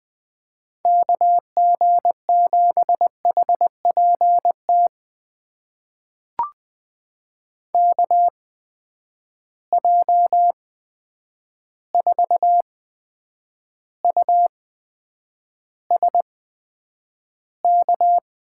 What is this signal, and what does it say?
Morse code, heard as a keyed tone, about 20 wpm, spaced as for 9 wpm, 700 Hz. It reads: KG7HPT KJ4USK
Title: USA Call Signs - 6 Characters - 8x Character Spacing - 20wpm